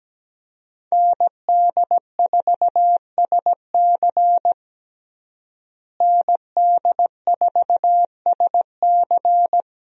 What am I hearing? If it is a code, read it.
Morse code, heard as a keyed tone, about 17 wpm, 700 Hz. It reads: ND4SC ND4SC
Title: USA Call Signs - 5 Characters - Repeated 2x - 17wpm